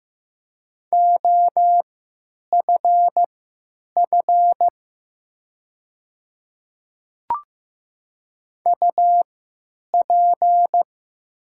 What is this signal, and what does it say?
Morse code, heard as a keyed tone, about 15 wpm, spaced as for 9 wpm, 700 Hz. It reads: OFF UP
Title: 3x Character Spacing - Top 200 Words - 15wpm